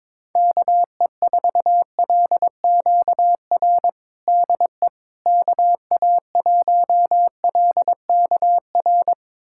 Morse code, heard as a keyed tone, about 22 wpm, 700 Hz. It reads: KE4LQR DE KA1LKR
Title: USA Callsign DE Callsign - 6 Characters - 22wpm